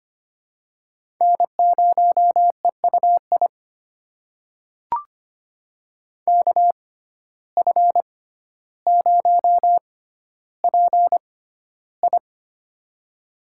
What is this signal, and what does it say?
Morse code, heard as a keyed tone, about 25 wpm, spaced as for 13 wpm, 700 Hz. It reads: N0EUI KF0PI
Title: USA Call Signs - 5 Characters - 6x Character Spacing - 25wpm